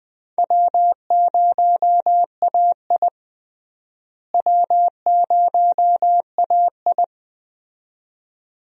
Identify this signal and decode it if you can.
Morse code, heard as a keyed tone, about 20 wpm, 700 Hz. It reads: W0AI W0AI
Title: USA Call Signs - 4 Characters - Repeated 2x - 20wpm